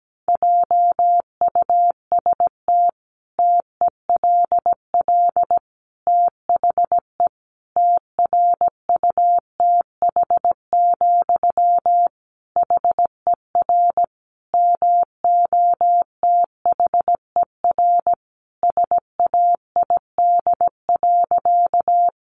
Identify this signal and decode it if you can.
Morse code, heard as a keyed tone, about 17 wpm, 700 Hz. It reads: JUST TELL THE TRUTH, HER MOTHER SAID.
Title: Sentences from Top 800 Words 17wpm